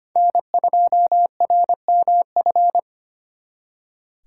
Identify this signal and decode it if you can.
Morse code, heard as a keyed tone, about 32 wpm, 700 Hz. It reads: N2RMF